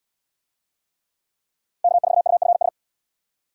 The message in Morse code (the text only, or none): K4UFS